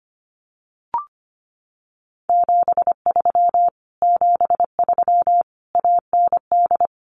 Morse code, heard as a keyed tone, about 25 wpm, 700 Hz. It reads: 73 73 AND
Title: Common QSO Phrases Encore 25wpm